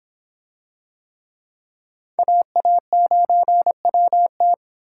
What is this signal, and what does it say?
Morse code, heard as a keyed tone, about 24 wpm, 700 Hz. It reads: AA9WT